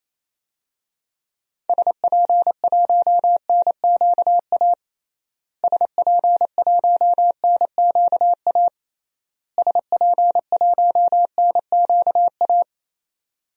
Morse code, heard as a keyed tone, about 28 wpm, 700 Hz. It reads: SP1NQA SP1NQA SP1NQA
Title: DXCC-100 Call Signs - 6 Characters - Repeated 3x - 28wpm